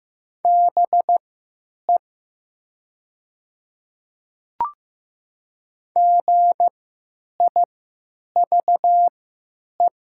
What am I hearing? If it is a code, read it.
Morse code, heard as a keyed tone, about 15 wpm, spaced as for 9 wpm, 700 Hz. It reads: BE GIVE
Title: 3x Character Spacing - Top 300 Words - 15wpm